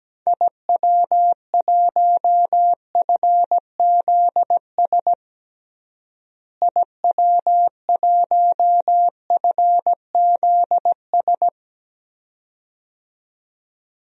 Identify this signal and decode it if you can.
Morse code, heard as a keyed tone, about 17 wpm, 700 Hz. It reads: IW1FZS IW1FZS
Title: DXCC-100 Call Signs - 6 Characters - Repeated 2x - 17wpm